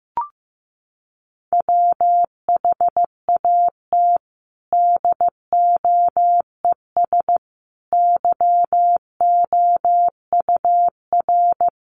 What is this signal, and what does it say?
Morse code, heard as a keyed tone, about 15 wpm, 700 Hz. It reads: WHAT DOES YOUR
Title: Sentences from Top 1500 Words 15wpm